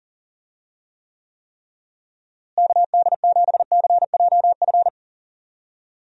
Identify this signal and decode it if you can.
Morse code, heard as a keyed tone, about 40 wpm, 700 Hz. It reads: KD7CJF
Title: USA Call Signs - 6 Characters - Repeated 3x - 40wpm